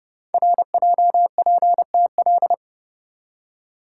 Morse code, heard as a keyed tone, about 28 wpm, 700 Hz. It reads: RJPTL
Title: Code Groups - 5 Letters - 30wpm